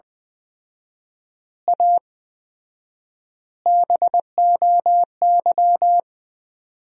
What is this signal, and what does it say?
Morse code, heard as a keyed tone, about 20 wpm, 700 Hz. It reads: A BOY
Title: 4x Word-Spacing - Sentences from Top 400 Words - 20wpm